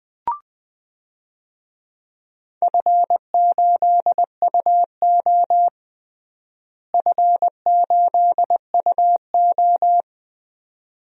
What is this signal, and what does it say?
Morse code, heard as a keyed tone, about 20 wpm, 700 Hz. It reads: F8UO F8UO
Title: DXCC-100 Call Signs - 4 Characters - Repeated 2x - 20wpm